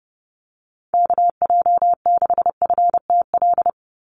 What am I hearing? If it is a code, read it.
Morse code, heard as a keyed tone, about 30 wpm, 700 Hz. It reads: KJ6FTL
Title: USA Call Signs 30wpm